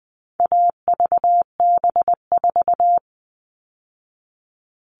Morse code, heard as a keyed tone, about 20 wpm, 700 Hz. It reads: AVB4
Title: All Cut Numbers - 4-Digit 20wpm